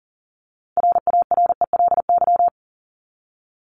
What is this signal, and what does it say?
Morse code, heard as a keyed tone, about 40 wpm, 700 Hz. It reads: RARELY
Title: LY Suffix - Top 100 Words with Base Words 40wpm